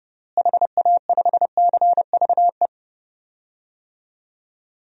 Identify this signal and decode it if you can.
Morse code, heard as a keyed tone, about 28 wpm, 700 Hz. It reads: HA5CVE
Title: DXCC-100 Call Signs - 6 Characters - Repeated 2x - 30wpm